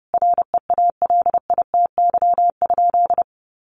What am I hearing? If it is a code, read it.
Morse code, heard as a keyed tone, about 30 wpm, 700 Hz. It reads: REALITY?
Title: Sentences from Top 1200 Words 30wpm